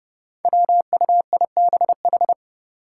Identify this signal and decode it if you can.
Morse code, heard as a keyed tone, about 32 wpm, 700 Hz. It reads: WUIBH
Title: Five Challenging Letter-Number Combinations 30wpm